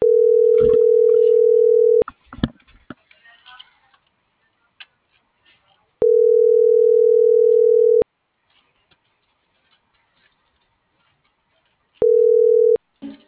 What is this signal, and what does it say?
Ringback tone (North America), 440 + 480 Hz.